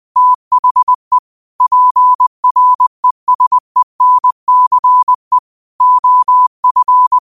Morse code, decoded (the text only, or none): THE PRESENCE OF